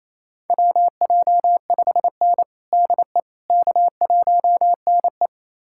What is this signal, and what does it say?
Morse code, heard as a keyed tone, about 28 wpm, 700 Hz. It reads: WJ5N DE K1NE
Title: USA Callsign DE Callsign - 4 Characters - 28wpm